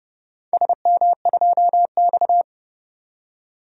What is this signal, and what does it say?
Morse code, heard as a keyed tone, about 30 wpm, 700 Hz. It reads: SM2X
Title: DXCC-100 Call Signs - 4 Characters - 30wpm